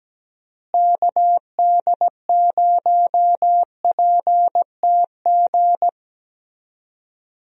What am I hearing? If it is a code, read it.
Morse code, heard as a keyed tone, about 17 wpm, 700 Hz. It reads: KD0PTG
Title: USA Call Signs - 6 Characters - Repeated 3x - 17wpm